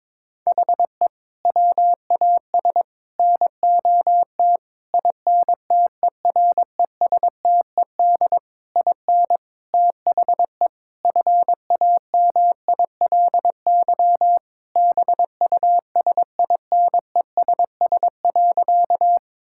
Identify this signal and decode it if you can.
Morse code, heard as a keyed tone, about 22 wpm, 700 Hz. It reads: HE WAS NOT INTERESTED IN THE FAMILY BUSINESS.
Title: Sentences from Top 1000 Words - Review Part 6 - 22wpm